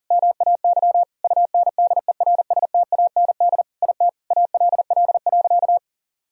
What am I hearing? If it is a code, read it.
Morse code, heard as a keyed tone, about 37 wpm, 700 Hz. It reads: MAY UNDERSTAND IT ALL.
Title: Sentences from Top 1000 Words - Review Part 4 - 40wpm